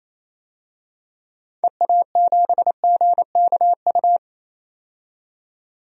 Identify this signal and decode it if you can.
Morse code, heard as a keyed tone, about 28 wpm, 700 Hz. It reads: EA7GKU
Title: DXCC-100 Call Signs - 6 Characters - 28wpm